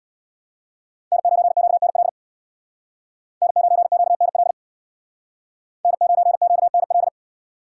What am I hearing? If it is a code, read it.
Morse code, heard as a keyed tone, about 62 wpm, 700 Hz. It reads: N9QNL N9QNL N9QNL
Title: Speed Racing - USA Call Signs 50wpm